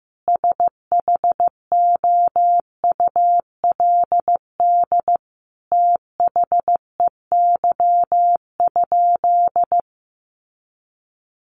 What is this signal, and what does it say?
Morse code, heard as a keyed tone, about 15 wpm, 700 Hz. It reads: SHOULD THEY?
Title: Sentences from Top 300 Words 15wpm